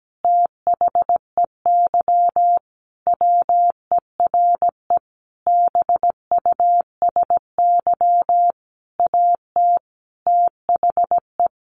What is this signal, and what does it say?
Morse code, heard as a keyed tone, about 17 wpm, 700 Hz. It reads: THEY WERE BUSY AT THE
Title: Sentences from Top 1500 Words 17wpm